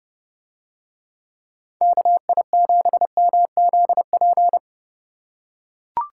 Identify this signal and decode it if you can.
Morse code, heard as a keyed tone, about 30 wpm, 700 Hz. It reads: KI7MZP